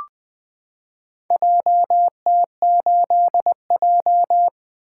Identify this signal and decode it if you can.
Morse code, heard as a keyed tone, about 21 wpm, 700 Hz. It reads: JT8J
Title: Four Letter-Number Combinations - 20wpm